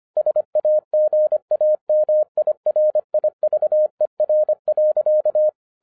Morse code, heard as a keyed tone, about 25 wpm, 610 Hz. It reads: SAGAMI RIVER.